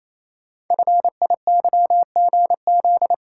Morse code, heard as a keyed tone, about 27 wpm, 700 Hz. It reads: FIYGZ
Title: Code Groups - 5 Letters - 28wpm